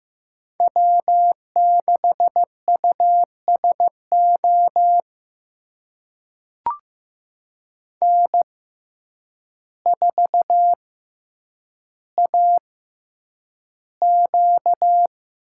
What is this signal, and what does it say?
Morse code, heard as a keyed tone, about 15 wpm, spaced as for 8 wpm, 700 Hz. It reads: W6USO N4AQ
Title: USA Call Signs - 6x Character Spacing - 15wpm